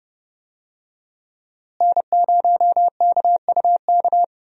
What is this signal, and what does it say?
Morse code, heard as a keyed tone, about 30 wpm, 700 Hz. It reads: N0KUK
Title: USA Call Signs - 5 Characters - Repeated 2x - 30wpm